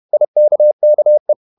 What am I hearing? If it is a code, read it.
Morse code, heard as a keyed tone, about 31 wpm, 600 Hz. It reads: IKKE
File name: Ferie QRQ 31wpm